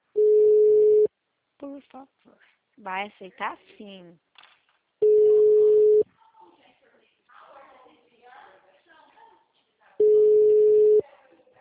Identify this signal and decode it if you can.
Ringback tone (Europe, 425 Hz), with speech.